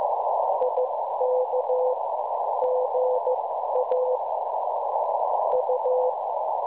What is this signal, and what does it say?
Morse code, heard as a keyed tone, about 14 wpm, spaced as for 12 wpm, 510 Hz, with noise. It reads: IK GA U
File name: cwronde / websdr recording .6kHz